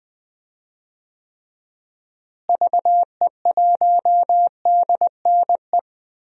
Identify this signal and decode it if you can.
Morse code, heard as a keyed tone, about 20 wpm, 690 Hz. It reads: VE1DNE
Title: DXCC-100 Call Signs - 6 Characters - Repeated 2x - 20wpm